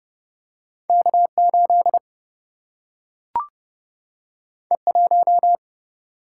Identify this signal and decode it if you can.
Morse code, heard as a keyed tone, about 29 wpm, 700 Hz. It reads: K 8 E 1
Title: Single-Letter Single-Number 30wpm